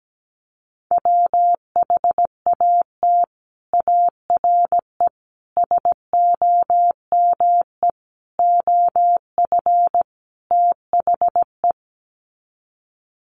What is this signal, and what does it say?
Morse code, heard as a keyed tone, about 17 wpm, 700 Hz. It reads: WHAT ARE SOME OF THE
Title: Sets of 5 Words Encore 17wpm